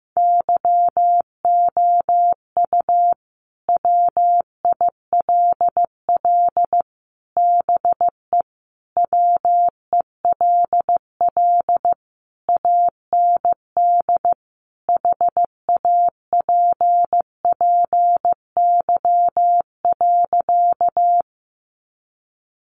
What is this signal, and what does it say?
Morse code, heard as a keyed tone, about 15 wpm, 700 Hz. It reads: YOU WILL BE WELL AND HAPPY.
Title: Sentences from Top 1000 Words 15wpm